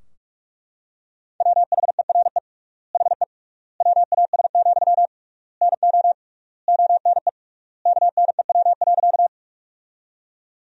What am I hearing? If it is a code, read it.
Morse code, heard as a keyed tone, about 42 wpm, 700 Hz. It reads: WHERE HE WAS, NO ONE KNEW.
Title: 3x Word-Spacing - Sentences from Top 400 Words - 45wpm